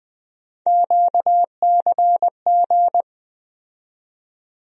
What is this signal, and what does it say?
Morse code, heard as a keyed tone, about 20 wpm, 700 Hz. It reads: QCG
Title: USA Call Sign Postfixes - 3 Letters 20wpm